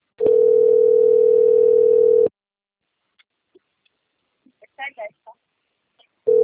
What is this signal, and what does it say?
Ringback tone (North America), 440 + 480 Hz.
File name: rg-1600-unknown- - - .54753